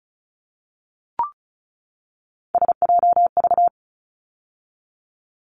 Morse code, heard as a keyed tone, about 34 wpm, 700 Hz. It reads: SJV
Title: ARRL Sections 35wpm